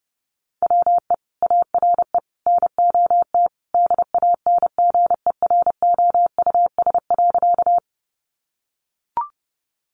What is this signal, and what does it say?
Morse code, heard as a keyed tone, about 30 wpm, 700 Hz. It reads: WE ARE NOT DANGEROUS.
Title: Sentences from Top 1200 Words 30wpm